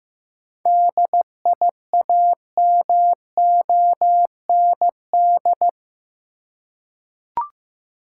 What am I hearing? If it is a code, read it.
Morse code, heard as a keyed tone, about 15 wpm, 700 Hz. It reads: DIAMOND
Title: Seven Letter Words - 15wpm